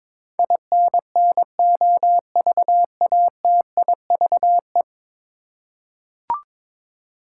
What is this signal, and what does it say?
Morse code, heard as a keyed tone, about 22 wpm, 700 Hz. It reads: INNOVATIVE
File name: practice / 22wpm / 10 Letter Words - First 3 Letters Given - 22wpm